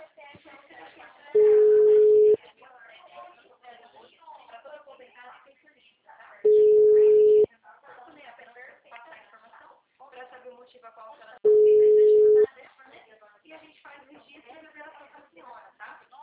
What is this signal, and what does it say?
Ringback tone (Europe, 425 Hz), with speech.